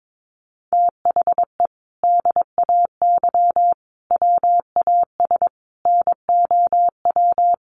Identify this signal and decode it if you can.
Morse code, heard as a keyed tone, about 22 wpm, 700 Hz. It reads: THE DAY WAS NOW
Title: Sentences from Top 1200 Words 22wpm